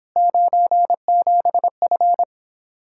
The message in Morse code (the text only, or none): 97F